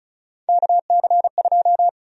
Morse code, heard as a keyed tone, about 36 wpm, 700 Hz. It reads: KC2
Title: USA Call Sign Prefixes - 35wpm